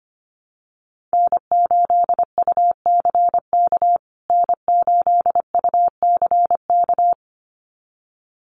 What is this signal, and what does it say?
Morse code, heard as a keyed tone, about 25 wpm, 700 Hz. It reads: N8UCK N8UCK
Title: USA Call Signs Repeated 2x 25wpm